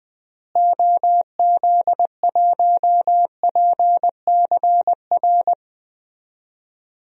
Morse code, heard as a keyed tone, about 20 wpm, 700 Hz. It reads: OZ1PCR
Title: DXCC-100 Call Signs - 6 Characters - Repeated 2x - 20wpm